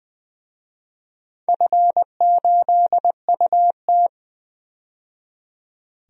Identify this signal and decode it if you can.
Morse code, heard as a keyed tone, about 20 wpm, 700 Hz. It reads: F8UT
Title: DXCC-100 Call Signs - 4 Characters - 20wpm